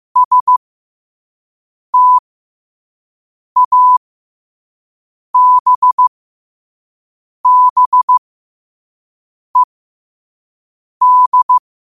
Morse code, heard as a keyed tone, about 15 wpm, spaced as for 5 wpm, 1000 Hz. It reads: STABBED